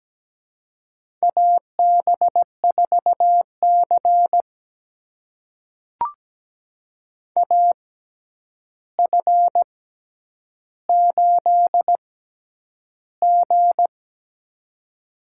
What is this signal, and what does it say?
Morse code, heard as a keyed tone, about 17 wpm, spaced as for 9 wpm, 700 Hz. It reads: AB4C AF8G
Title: USA Call Signs - 4 Characters - 6x Character Spacing - 17wpm